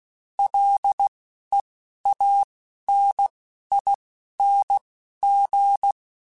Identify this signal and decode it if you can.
Morse code, heard as a keyed tone, about 16 wpm, spaced as for 12 wpm, 800 Hz. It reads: LEANING
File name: Combinations / Combinations-mp3s / Combinations@10